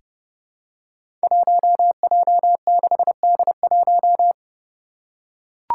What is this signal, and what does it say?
Morse code, heard as a keyed tone, about 30 wpm, 700 Hz. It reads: 1J6D1
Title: Five Challenging Letter-Number Combinations 30wpm